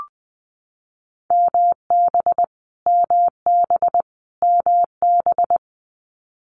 Morse code, heard as a keyed tone, about 20 wpm, 700 Hz. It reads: MB MB MB
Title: Canadian Provinces and Territories - Repeated 3x - 20wpm